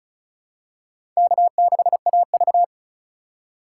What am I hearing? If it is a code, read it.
Morse code, heard as a keyed tone, about 34 wpm, 700 Hz. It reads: K6AV
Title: USA Call Signs - 4 Characters - 3x Character Spacing - 35wpm